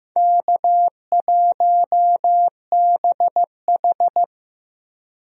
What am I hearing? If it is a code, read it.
Morse code, heard as a keyed tone, about 15 wpm, 700 Hz. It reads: K1BH
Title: USA Call Signs - 4 Characters - Repeated 2x - 15wpm